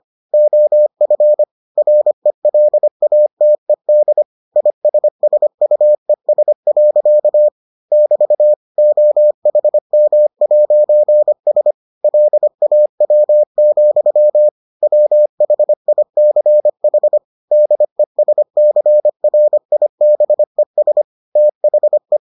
Morse code, heard as a keyed tone, about 25 wpm, 600 Hz. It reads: OF RELATED ISSUES. <BT> OHM'S LAW, WHICH DESCRIBES THE